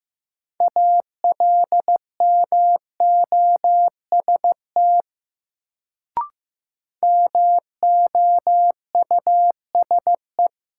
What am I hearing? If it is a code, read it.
Morse code, heard as a keyed tone, about 15 wpm, 700 Hz. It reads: ALMOST MOUSE
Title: Sentences from Top 700 Words - New Word Drill - 15wpm